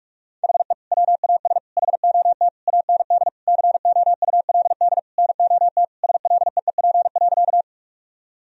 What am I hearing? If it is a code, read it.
Morse code, heard as a keyed tone, about 46 wpm, 700 Hz. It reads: HE WAS HOT AND COULD NOT SLEEP.
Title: Sentences from Top 1000 Words - Review Part 3 - 45wpm